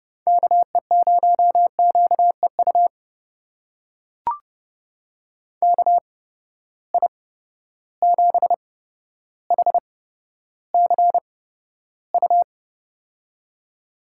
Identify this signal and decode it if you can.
Morse code, heard as a keyed tone, about 30 wpm, spaced as for 13 wpm, 700 Hz. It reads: KE0QEU KI7HCU
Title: USA Call Signs - 6 Characters - 8x Character Spacing - 30wpm